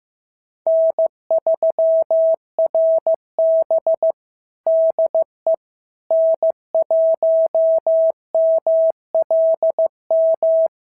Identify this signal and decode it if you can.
Morse code, heard as a keyed tone, about 14 wpm, 650 Hz. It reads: N3RB DE N1MLM
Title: SOTA QSO Practice 15wpm